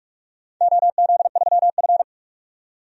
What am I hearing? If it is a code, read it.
Morse code, heard as a keyed tone, about 46 wpm, 700 Hz. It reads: OZ3F
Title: DXCC-100 Call Signs - 4 Characters - 45wpm